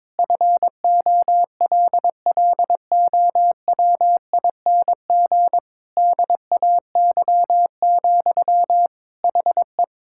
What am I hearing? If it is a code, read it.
Morse code, heard as a keyed tone, about 23 wpm, 700 Hz. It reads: FOLLOWING DAY, HE
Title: Sentences from Top 1000 Words - Review Part 6 - 22wpm